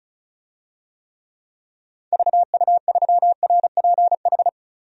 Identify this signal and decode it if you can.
Morse code, heard as a keyed tone, about 36 wpm, 700 Hz. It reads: VU3RPH
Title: DXCC-100 Call Signs - 6 Characters - 35wpm